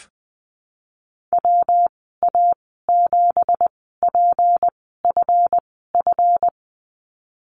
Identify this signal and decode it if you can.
Morse code, heard as a keyed tone, about 20 wpm, spaced as for 15 wpm, 700 Hz. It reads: WA7PFF
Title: 238 Lesson 38g - Letter q - Call Signs 20wpm